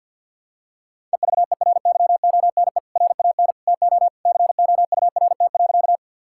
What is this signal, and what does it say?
Morse code, heard as a keyed tone, about 51 wpm, 700 Hz. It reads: EVERYONE RAN TO COURT.
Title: Sentences from Top 1600 Words 50wpm